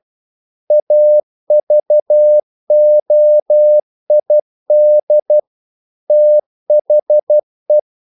Morse code, heard as a keyed tone, about 12 wpm, 600 Hz. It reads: AVOID THE
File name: CW / Practice / EnglishQuotations0112